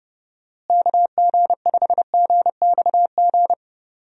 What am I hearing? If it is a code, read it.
Morse code, heard as a keyed tone, about 30 wpm, 700 Hz. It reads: KG5GXG
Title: USA Call Signs - 6 Characters - Repeated 2x - 30wpm